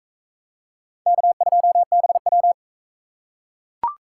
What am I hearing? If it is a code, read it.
Morse code, heard as a keyed tone, about 42 wpm, 700 Hz. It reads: K2BW